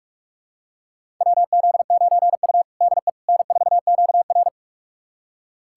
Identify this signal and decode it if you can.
Morse code, heard as a keyed tone, about 46 wpm, 700 Hz. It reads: WZ9U DE N4QR
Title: USA Callsign DE Callsign - 4 Characters - 45wpm